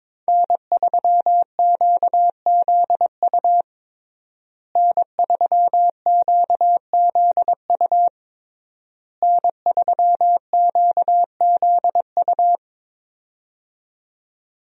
Morse code, heard as a keyed tone, about 22 wpm, 700 Hz. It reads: N3QZU N3QZU N3QZU
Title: USA Call Signs - 5 Characters - Repeated 3x - 22wpm